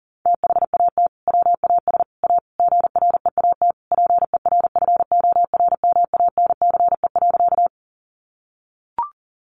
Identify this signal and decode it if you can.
Morse code, heard as a keyed tone, about 40 wpm, 700 Hz. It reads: THAT WAS A GREAT PERFORMANCE.
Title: Sentences from Top 1000 Words 40wpm